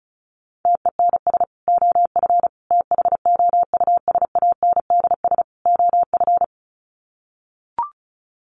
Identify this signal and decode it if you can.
Morse code, heard as a keyed tone, about 34 wpm, 700 Hz. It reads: TENS OF THOUSANDS OF
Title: Sets of 4 Words 35wpm